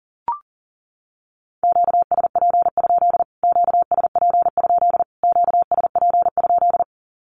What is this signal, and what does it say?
Morse code, heard as a keyed tone, about 40 wpm, 700 Hz. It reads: QSP? QSP? QSP?
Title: Q-Codes Repeated 3x 40wpm